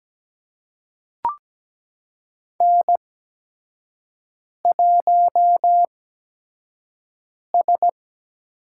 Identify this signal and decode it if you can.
Morse code, heard as a keyed tone, about 17 wpm, spaced as for 5 wpm, 700 Hz. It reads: N1S
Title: USA Call Signs - 5 Characters - 8x Character Spacing - 17wpm